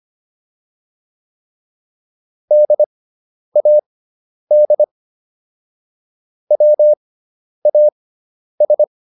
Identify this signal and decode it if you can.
Morse code, heard as a keyed tone, about 25 wpm, spaced as for 10 wpm, 600 Hz. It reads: DAD WAS